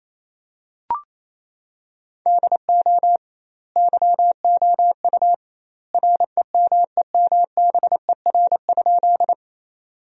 Morse code, heard as a keyed tone, about 28 wpm, 700 Hz. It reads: DO YOU REMEMBER?
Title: 2x Word-Spacing - Sentences from Top 400 Words - 28wpm